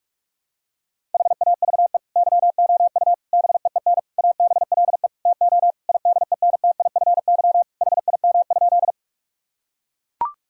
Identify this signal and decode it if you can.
Morse code, heard as a keyed tone, about 41 wpm, 700 Hz. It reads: HAVE YOU BEEN ABLE TO IDENTIFY HIM?
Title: Sentences from Top 1000 Words - Review Part 6 - 45wpm